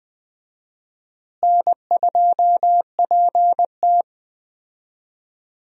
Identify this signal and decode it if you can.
Morse code, heard as a keyed tone, about 20 wpm, 700 Hz. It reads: N2PT
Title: USA Call Signs - 4 Characters - Repeated 2x - 20wpm